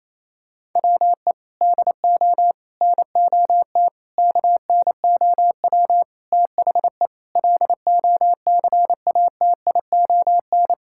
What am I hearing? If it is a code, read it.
Morse code, heard as a keyed tone, about 28 wpm, 700 Hz. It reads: WE DO NOT KNOW THE LOCATION
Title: Sentences from Top 1000 Words - Review Part 6 - 28wpm